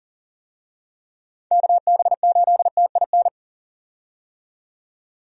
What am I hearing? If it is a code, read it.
Morse code, heard as a keyed tone, about 40 wpm, 700 Hz. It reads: KB8TIN
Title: USA Call Signs - 6 Characters - 40wpm